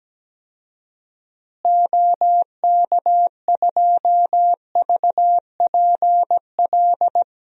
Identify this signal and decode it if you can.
Morse code, heard as a keyed tone, about 17 wpm, 700 Hz. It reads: OK2VPL
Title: DXCC-100 Call Signs - 6 Characters - Repeated 2x - 17wpm